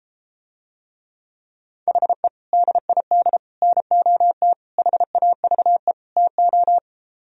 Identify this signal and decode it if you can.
Morse code, heard as a keyed tone, about 33 wpm, 700 Hz. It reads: HE DID NOT HAVE TO